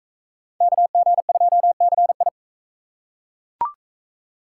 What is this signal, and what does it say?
Morse code, heard as a keyed tone, about 39 wpm, 700 Hz. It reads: KG2CI